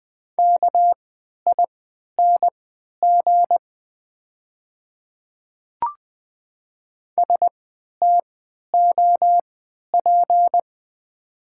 Morse code, heard as a keyed tone, about 20 wpm, spaced as for 11 wpm, 700 Hz. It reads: KING STOP
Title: 3x Character Spacing - Top 500 Words - 20wpm